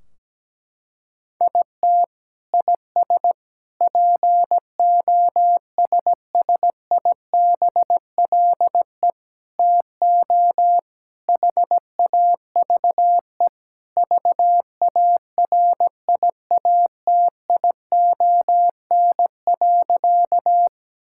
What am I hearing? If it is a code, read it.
Morse code, heard as a keyed tone, about 17 wpm, 700 Hz. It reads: IT IS POSSIBLE TO HAVE VARIATION.
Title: Sentences from Top 1600 Words 17wpm